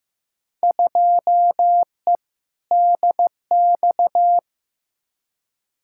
Morse code, heard as a keyed tone, about 15 wpm, 700 Hz. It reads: 2E DX
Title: Field Day Exchanges - 15wpm